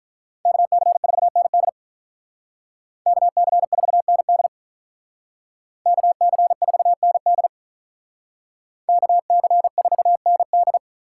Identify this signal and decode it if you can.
Morse code, heard as a keyed tone, about 44 wpm, 700 Hz. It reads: KC4ND KC4ND KC4ND KC4ND